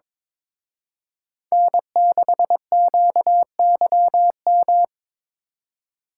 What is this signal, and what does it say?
Morse code, heard as a keyed tone, about 26 wpm, 700 Hz. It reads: N6QYM
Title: Speed Racing - USA Call Signs 22wpm